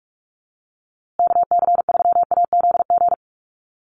Morse code, heard as a keyed tone, about 44 wpm, 700 Hz. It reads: KC3AZG
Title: USA Call Signs Repeated 2x 45wpm